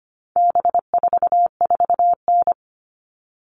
Morse code, heard as a keyed tone, about 25 wpm, 700 Hz. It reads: B44N
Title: All Cut Numbers - 4-Digit 25wpm